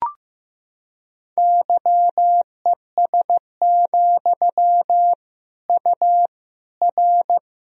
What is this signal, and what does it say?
Morse code, heard as a keyed tone, about 16 wpm, 700 Hz. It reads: YES, U R
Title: Sentences from Top 500 Words and Abbreviations 15wpm